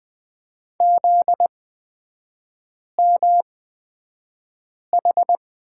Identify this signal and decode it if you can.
Morse code, heard as a keyed tone, about 20 wpm, spaced as for 5 wpm, 700 Hz. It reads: ZMH
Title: 243 Lesson 39e - Letter z - Three Characters 20wpm